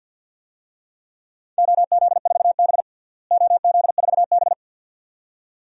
Morse code, heard as a keyed tone, about 52 wpm, 700 Hz. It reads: OZ4B OZ4B
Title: DXCC-100 Call Signs - 4 Characters - Repeated 2x - 50wpm